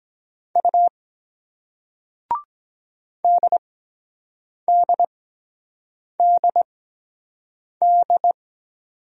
Morse code, heard as a keyed tone, about 23 wpm, spaced as for 7 wpm, 700 Hz. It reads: U DDDD